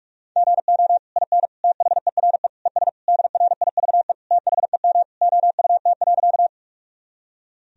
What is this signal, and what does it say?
Morse code, heard as a keyed tone, about 41 wpm, 700 Hz. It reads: GO IN THERE ES DRIVE THEM OUT.
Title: Sentences from Top 500 Words and Abbreviations 45wpm